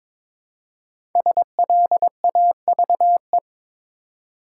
Morse code, heard as a keyed tone, about 22 wpm, 700 Hz. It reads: SLAVE
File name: 5 Letter Words - First 3 Letters Given - 22wpm